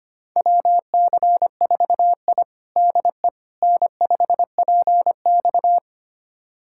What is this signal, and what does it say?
Morse code, heard as a keyed tone, about 25 wpm, 700 Hz. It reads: WC4I DE N5PX
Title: USA Callsign DE Callsign - 4 Characters - 25wpm